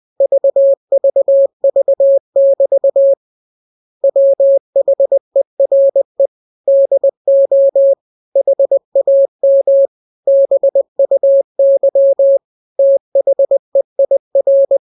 Morse code, heard as a keyed tone, about 20 wpm, 550 Hz. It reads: VVV<BT> WHERE DO HAM BUY THEIR
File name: jokes / joke 19 20 0000